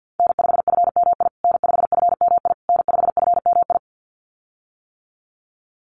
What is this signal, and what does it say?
Morse code, heard as a keyed tone, about 50 wpm, 700 Hz. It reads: N5FMI N5FMI N5FMI
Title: USA Call Signs Repeated 3x 50wpm